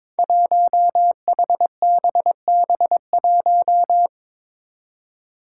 Morse code, heard as a keyed tone, about 22 wpm, 700 Hz. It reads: 1HBB1
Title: Five Challenging Letter-Number Combinations 22wpm